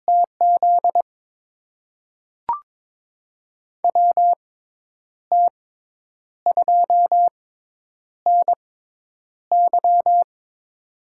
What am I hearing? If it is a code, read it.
Morse code, heard as a keyed tone, about 21 wpm, spaced as for 9 wpm, 700 Hz. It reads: TZ WT2NY
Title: USA Call Signs - 6x Character Spacing - 22wpm